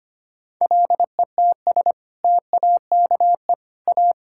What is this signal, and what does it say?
Morse code, heard as a keyed tone, about 25 wpm, 700 Hz. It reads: LETS TAKE A
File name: practice / 25wpm / Sentences with 2-Word Phrases - 25wpm